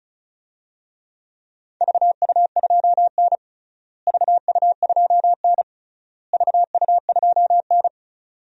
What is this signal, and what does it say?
Morse code, heard as a keyed tone, about 36 wpm, 700 Hz. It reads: VU2N VU2N VU2N
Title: DXCC-100 Call Signs - 4 Characters - Repeated 3x - 35wpm